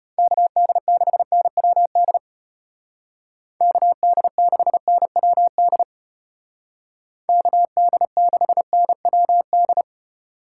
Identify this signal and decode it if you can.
Morse code, heard as a keyed tone, about 33 wpm, 700 Hz. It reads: KD6NWD KD6NWD KD6NWD